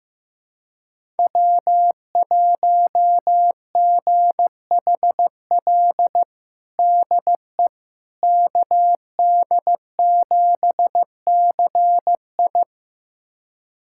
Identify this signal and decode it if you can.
Morse code, heard as a keyed tone, about 16 wpm, 700 Hz. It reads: W1GHL DE KD7CI
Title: USA Callsign DE Callsign - 5 Characters - 15wpm